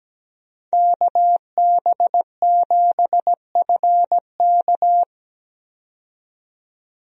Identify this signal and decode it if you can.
Morse code, heard as a keyed tone, about 17 wpm, 700 Hz. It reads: KB7FK
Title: USA Call Signs - 5 Characters - 17wpm